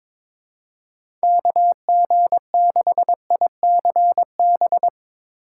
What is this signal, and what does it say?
Morse code, heard as a keyed tone, about 22 wpm, 700 Hz. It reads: KG6ICB
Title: USA Call Signs - 6 Characters - 8x Character Spacing - 22wpm